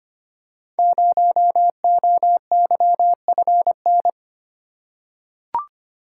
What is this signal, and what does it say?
Morse code, heard as a keyed tone, about 25 wpm, 700 Hz. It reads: 0OYFN
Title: Code Groups - 5 Letter:Numbers - 25wpm